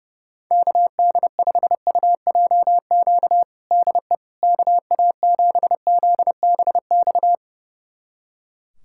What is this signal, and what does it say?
Morse code, heard as a keyed tone, about 30 wpm, 700 Hz. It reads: KD5UJQ DE KA7ZBX
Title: USA Callsign DE Callsign - 6 Characters - 30wpm